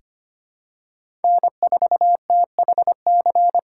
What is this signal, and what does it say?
Morse code, heard as a keyed tone, about 25 wpm, 700 Hz. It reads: N4THC
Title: USA Call Signs - 5 Characters - 2x Character Spacing - 25wpm